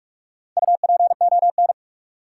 Morse code, heard as a keyed tone, about 44 wpm, 700 Hz. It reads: UPON
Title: Sentences from Top 1000 Words - Review Part 6 - New Word Drill - 45wpm